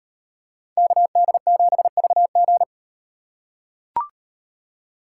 Morse code, heard as a keyed tone, about 38 wpm, 700 Hz. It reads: KD7VG